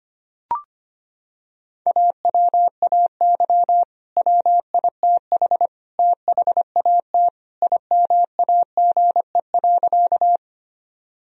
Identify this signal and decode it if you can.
Morse code, heard as a keyed tone, about 25 wpm, 700 Hz. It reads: AWAY WITH THAT IMAGE.
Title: Sentences from Top 1000 Words - Review Part 4 - 25wpm